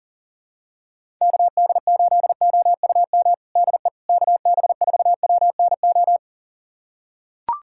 Morse code, heard as a keyed tone, about 40 wpm, 700 Hz. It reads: KD8OUM DE KB4WNO
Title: USA Callsign DE Callsign 40wpm